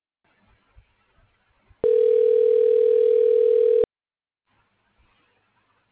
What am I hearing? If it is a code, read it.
Ringback tone (North America), 440 + 480 Hz.